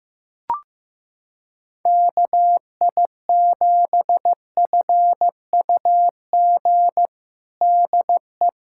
Morse code, heard as a keyed tone, about 15 wpm, 700 Hz. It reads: KI7FUG DE
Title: USA Callsign DE Callsign - 6 Characters - 15wpm